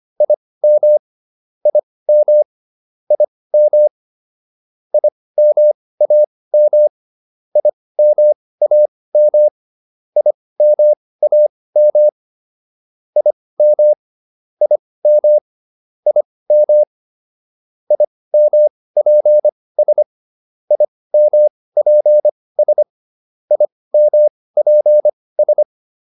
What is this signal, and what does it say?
Morse code, heard as a keyed tone, about 25 wpm, spaced as for 18 wpm, 600 Hz. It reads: IM IM IM IMAM IMAM IMAM IM IM IM IMPS IMPS IMPS